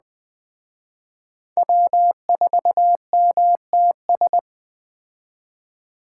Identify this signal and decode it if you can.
Morse code, heard as a keyed tone, about 20 wpm, 700 Hz. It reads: W4MTS
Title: Speed Racing - USA 1x3 Call Signs 17wpm